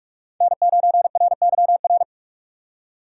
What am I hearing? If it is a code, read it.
Morse code, heard as a keyed tone, about 43 wpm, 700 Hz. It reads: N9RYR
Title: USA Call Signs - 5 Characters - 6x Character Spacing - 45wpm